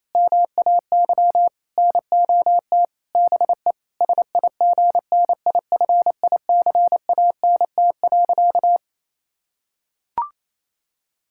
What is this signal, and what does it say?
Morse code, heard as a keyed tone, about 27 wpm, 700 Hz. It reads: MAY NOT BE SIGNIFICANT.
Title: Sentences from Top 1000 Words - Review Part 6 - 28wpm